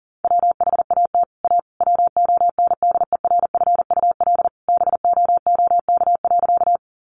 Morse code, heard as a keyed tone, about 40 wpm, 700 Hz. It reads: WHAT A WONDERFUL BOOK.
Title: Sentences from Top 1100 Words 40wpm